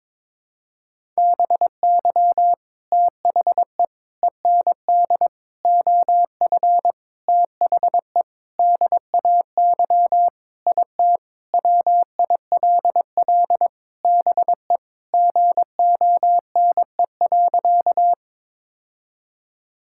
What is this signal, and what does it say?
Morse code, heard as a keyed tone, about 22 wpm, 700 Hz. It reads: BY THE END OF THE DAY IT WILL BE GONE.
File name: practice / 22wpm / Sentences with 4-Word Phrases - 22wpm